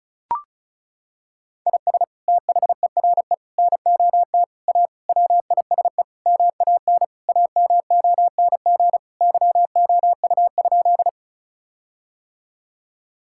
Morse code, heard as a keyed tone, about 36 wpm, 700 Hz. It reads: IS THERE NOT A WISE MAN AMONG YOU?
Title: Sentences from Top 1600 Words 35wpm